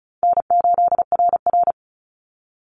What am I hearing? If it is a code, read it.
Morse code, heard as a keyed tone, about 36 wpm, 700 Hz. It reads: N8RR
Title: Call Signs - 35wpm